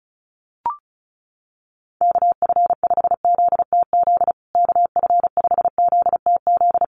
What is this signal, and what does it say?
Morse code, heard as a keyed tone, about 34 wpm, 700 Hz. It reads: KF5ZTZ KF5ZTZ
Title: USA Call Signs Repeated 3x 35wpm